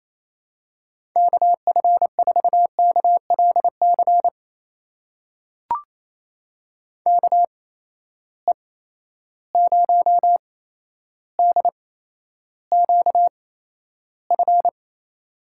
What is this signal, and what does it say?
Morse code, heard as a keyed tone, about 28 wpm, spaced as for 12 wpm, 700 Hz. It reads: KF4KLC KE0DQF
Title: USA Call Signs - 6 Characters - 8x Character Spacing - 28wpm